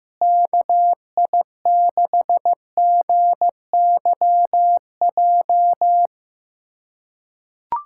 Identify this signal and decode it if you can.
Morse code, heard as a keyed tone, about 15 wpm, 700 Hz. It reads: KI6GYJ
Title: USA Callsign DE Callsign 15wpm